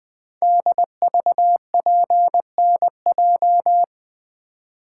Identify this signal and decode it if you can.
Morse code, heard as a keyed tone, about 20 wpm, 700 Hz. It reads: DVPNJ
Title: Code Groups - 5 Letters - 20wpm